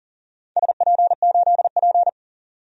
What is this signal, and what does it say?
Morse code, heard as a keyed tone, about 40 wpm, 700 Hz. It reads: SP8P
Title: DXCC-100 Call Signs - 4 Characters - 40wpm